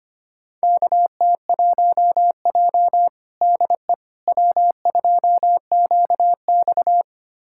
Morse code, heard as a keyed tone, about 25 wpm, 700 Hz. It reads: KT1J DE W2QX
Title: USA Callsign DE Callsign - 4 Characters - 25wpm